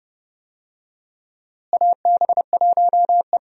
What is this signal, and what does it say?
Morse code, heard as a keyed tone, about 30 wpm, 700 Hz. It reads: AB1E
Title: USA Call Signs - 4 Characters - Repeated 3x - 30wpm